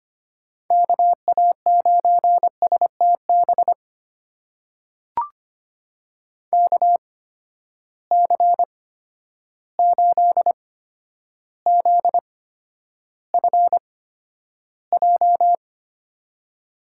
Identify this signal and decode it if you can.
Morse code, heard as a keyed tone, about 25 wpm, spaced as for 11 wpm, 700 Hz. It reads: KA9STB KC8ZFJ
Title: USA Call Signs - 6 Characters - 8x Character Spacing - 25wpm